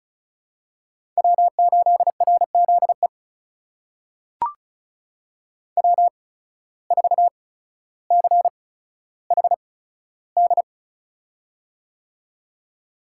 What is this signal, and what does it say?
Morse code, heard as a keyed tone, about 36 wpm, spaced as for 15 wpm, 700 Hz. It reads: W8RZE W4CHD
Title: USA Call Signs - 5 Characters - 8x Character Spacing - 35wpm